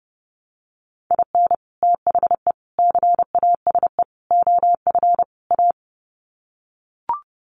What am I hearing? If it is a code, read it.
Morse code, heard as a keyed tone, about 30 wpm, 700 Hz. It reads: IN THE CASE OF A
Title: Sets of 5 Words Encore 30wpm